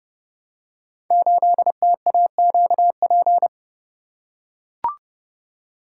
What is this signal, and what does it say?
Morse code, heard as a keyed tone, about 30 wpm, 700 Hz. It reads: 8TAQP
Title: Code Groups - 5 Letter:Numbers - 30wpm